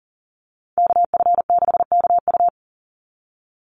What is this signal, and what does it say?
Morse code, heard as a keyed tone, about 40 wpm, 700 Hz. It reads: KF6KU
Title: USA Call Signs 40wpm